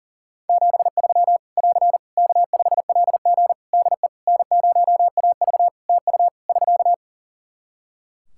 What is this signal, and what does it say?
Morse code, heard as a keyed tone, about 40 wpm, 700 Hz. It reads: 73 <AR> K5LG DE N0AV TU <SK>
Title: K1USN SST QSO Practice 40wpm